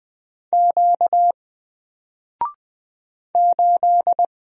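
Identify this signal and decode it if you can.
Morse code, heard as a keyed tone, about 20 wpm, 700 Hz. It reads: Q 8
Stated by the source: Single letter and number Morse drill at 20 wpm, from N4LCD.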